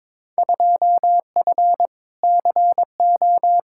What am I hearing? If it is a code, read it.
Morse code, heard as a keyed tone, about 22 wpm, 700 Hz. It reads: 2F CO
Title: Field Day Exchanges - 22wpm